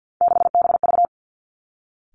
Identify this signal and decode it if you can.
Morse code, heard as a keyed tone, about 50 wpm, 700 Hz. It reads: 6BV
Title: Three Challenging Letter-Number Combinations 50wpm